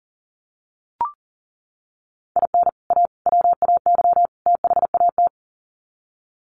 Morse code, heard as a keyed tone, about 38 wpm, 700 Hz. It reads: IN A WAY THAT
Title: Sets of 4 Words 40wpm